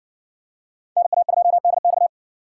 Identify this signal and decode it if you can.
Morse code, heard as a keyed tone, about 60 wpm, 700 Hz. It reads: NA2DX